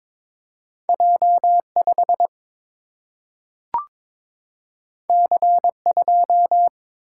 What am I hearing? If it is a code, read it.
Morse code, heard as a keyed tone, about 22 wpm, 700 Hz. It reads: J 5 C 2
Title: Single-Letter Single-Number 22wpm